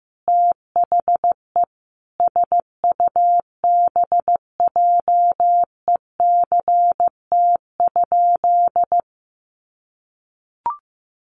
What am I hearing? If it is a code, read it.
Morse code, heard as a keyed tone, about 15 wpm, 700 Hz. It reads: THE SUBJECT?
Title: Sentences from Top 800 Words 15wpm